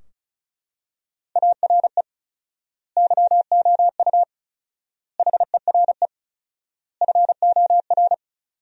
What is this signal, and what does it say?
Morse code, heard as a keyed tone, about 34 wpm, 700 Hz. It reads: ARE YOU HERE FOR
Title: 4x Word-Spacing - Sentences from Top 200 Words - 35wpm